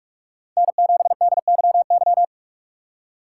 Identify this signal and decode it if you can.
Morse code, heard as a keyed tone, about 45 wpm, 700 Hz. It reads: N7DYY
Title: USA Call Signs - 5 Characters - 2x Character Spacing - 45wpm